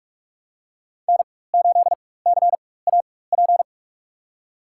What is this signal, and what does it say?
Morse code, heard as a keyed tone, about 47 wpm, spaced as for 21 wpm, 700 Hz. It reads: N8CAP
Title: USA Call Signs - 5 Characters - 4x Character Spacing - 45wpm